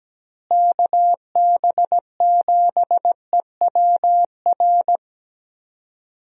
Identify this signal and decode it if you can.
Morse code, heard as a keyed tone, about 17 wpm, 700 Hz. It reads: KB7EWR
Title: USA Call Signs - 6 Characters - Repeated 2x - 17wpm